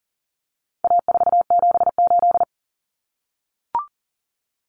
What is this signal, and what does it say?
Morse code, heard as a keyed tone, about 40 wpm, 700 Hz. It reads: A478
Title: Common Cut Numbers - 4-Digit 40wpm